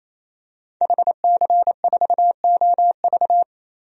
Morse code, heard as a keyed tone, about 28 wpm, 700 Hz. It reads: HC4OV
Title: DXCC-100 Call Signs - 5 Characters - 28wpm